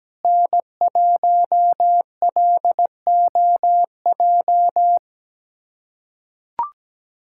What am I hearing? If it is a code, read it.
Morse code, heard as a keyed tone, about 17 wpm, 700 Hz. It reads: N1LOJ
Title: USA Callsign DE Callsign - 5 Characters - 17wpm